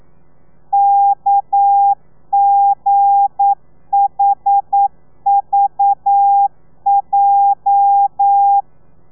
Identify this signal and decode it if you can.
Morse code, heard as a keyed tone, about 9 wpm, 800 Hz. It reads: KGHVJ
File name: telegrafia / i2tfj / Lez8 40